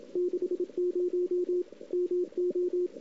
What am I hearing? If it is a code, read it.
Morse code, heard as a keyed tone, about 29 wpm, 380 Hz. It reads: 60 MO
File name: CW / 2008-ss-cw-k5tr-op-n5rz -02:00